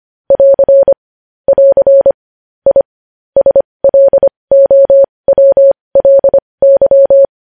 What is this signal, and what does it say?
Morse code, heard as a keyed tone, about 25 wpm, spaced as for 20 wpm, 550 Hz. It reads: <AR> <AR> I SLOWLY
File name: podcast / News headlines-1x-25x20